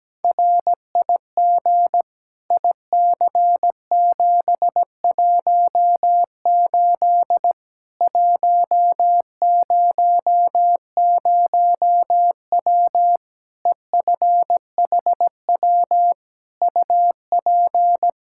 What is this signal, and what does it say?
Morse code, heard as a keyed tone, about 17 wpm, 700 Hz. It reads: RIG IC718 100W EFHW UP
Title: Practice QSO - Exchange 2 - Info - 17wpm